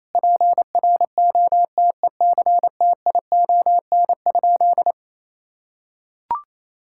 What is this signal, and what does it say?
Morse code, heard as a keyed tone, about 28 wpm, 700 Hz. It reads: PROTECTION?
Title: Sentences from Top 1600 Words 28wpm